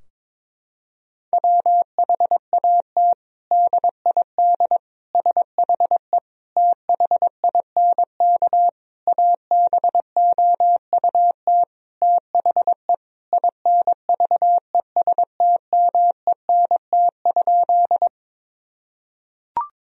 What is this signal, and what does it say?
Morse code, heard as a keyed tone, about 21 wpm, 700 Hz. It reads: WHAT DID SHE THINK ABOUT THE INVESTMENT?
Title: Sentences from Top 1000 Words - Review Part 4 - 22wpm